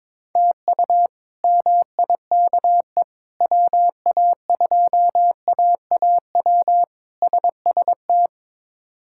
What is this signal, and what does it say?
Morse code, heard as a keyed tone, about 22 wpm, 700 Hz. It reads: TU MIKE WA2AAW SST
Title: K1USN SST QSO Practice - 22wpm